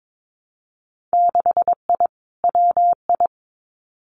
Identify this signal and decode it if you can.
Morse code, heard as a keyed tone, about 22 wpm, 700 Hz. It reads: 6I WI
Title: Winter Field Day Exchanges Encore 22wpm